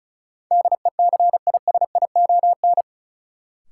Morse code, heard as a keyed tone, about 36 wpm, 700 Hz. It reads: DECISION
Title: Sentences from Top 1000 Words - New Word Drill - 35wpm